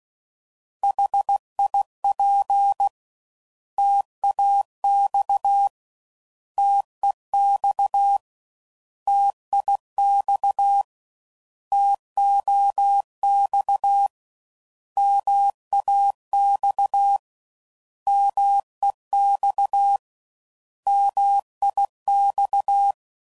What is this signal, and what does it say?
Morse code, heard as a keyed tone, about 16 wpm, 800 Hz. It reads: HIP TAX TEX TIX TOX MAX MEX MIX
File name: Mixed / Contrasts / 3LtrContrasts